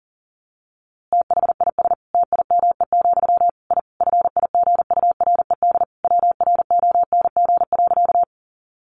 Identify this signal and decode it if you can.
Morse code, heard as a keyed tone, about 40 wpm, 700 Hz. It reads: THIS TIME, I FIGURED WRONG.
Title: A Sound Of Thunder 40wpm